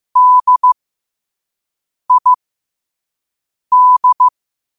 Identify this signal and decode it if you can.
Morse code, heard as a keyed tone, about 15 wpm, spaced as for 5 wpm, 1000 Hz. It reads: DID
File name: qotdfiles / 5wpm